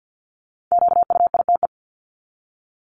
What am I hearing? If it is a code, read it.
Morse code, heard as a keyed tone, about 50 wpm, 700 Hz. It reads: QUITE
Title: Top 501-750 Words 50wpm